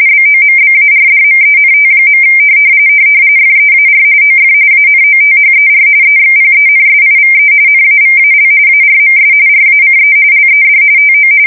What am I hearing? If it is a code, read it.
RTTY, about 45.45 baud. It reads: BOMBARDMENT. FEAR OF ANOTHER EXPENSIVE NAVAL ARMS RACE LED TO
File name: fsk170 / USS-New-Jersey